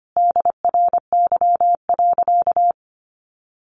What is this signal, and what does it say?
Morse code, heard as a keyed tone, about 25 wpm, 700 Hz. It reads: DRY.
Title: Sentences from Top 1500 Words 25wpm